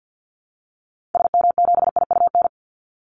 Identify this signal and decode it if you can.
Morse code, heard as a keyed tone, about 48 wpm, 700 Hz. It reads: SM7IUN
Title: Call Signs Repeated 3x - 50wpm